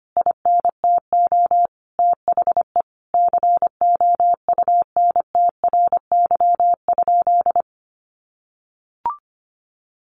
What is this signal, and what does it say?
Morse code, heard as a keyed tone, about 25 wpm, 700 Hz. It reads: INTO THE COUNTRY?
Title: Sentences from Top 1400 Words 25wpm